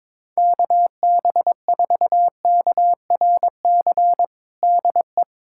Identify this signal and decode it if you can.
Morse code, heard as a keyed tone, about 22 wpm, 700 Hz. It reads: KB4KRC DE
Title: USA Callsign DE Callsign - 6 Characters - 22wpm